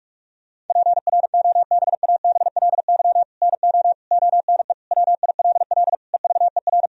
Morse code, heard as a keyed tone, about 46 wpm, 700 Hz. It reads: PROBABLY NO ONE WILL EVER
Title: Sentences from Top 1000 Words - Review Part 4 - 45wpm